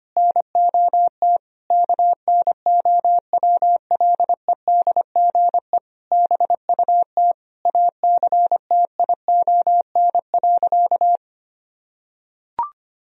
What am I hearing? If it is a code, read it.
Morse code, heard as a keyed tone, about 25 wpm, 700 Hz. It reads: NOT KNOWLEDGE BUT ACTION.
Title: Sentences from Top 1000 Words - Review Part 4 - 25wpm